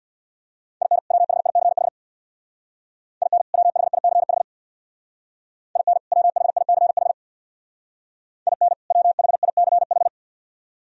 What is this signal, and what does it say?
Morse code, heard as a keyed tone, about 59 wpm, 700 Hz. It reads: IN WHICH IN WHICH IN WHICH IN WHICH
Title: Speed-Racing - Sets of 2 Words 50wpm